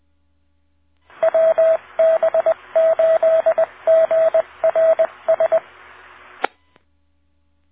Morse code, heard as a keyed tone, about 20 wpm, 650 Hz, with radio noise. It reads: WB8GRS